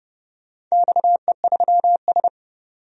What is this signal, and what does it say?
Morse code, heard as a keyed tone, about 30 wpm, 700 Hz. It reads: XE3S
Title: DXCC-100 Call Signs - 4 Characters - 30wpm